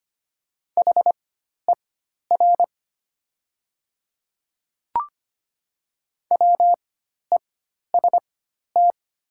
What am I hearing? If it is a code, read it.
Morse code, heard as a keyed tone, about 25 wpm, spaced as for 12 wpm, 700 Hz. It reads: HER WEST
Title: 4x Character Spacing - Top 500 Words - 25wpm